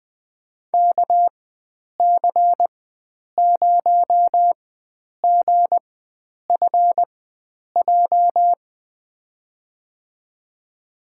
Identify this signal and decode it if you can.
Morse code, heard as a keyed tone, about 20 wpm, spaced as for 9 wpm, 700 Hz. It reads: KC0GFJ
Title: USA Call Signs - 6 Characters - 4x Character Spacing - 20wpm